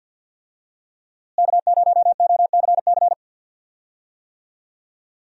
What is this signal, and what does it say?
Morse code, heard as a keyed tone, about 51 wpm, 700 Hz. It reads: K0OCC
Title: USA Call Signs - 5 Characters - Repeated 2x - 50wpm